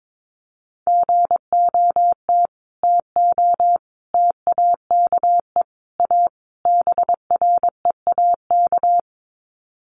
Morse code, heard as a keyed tone, about 22 wpm, 700 Hz. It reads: GOT TO TAKE A BREAK
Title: Sets of 5 Words Encore 22wpm